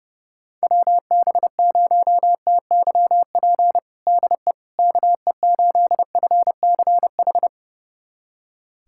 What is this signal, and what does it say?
Morse code, heard as a keyed tone, about 32 wpm, 700 Hz. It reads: WB0TYP DE KE8FCH
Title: USA Callsign DE Callsign - 6 Characters - 30wpm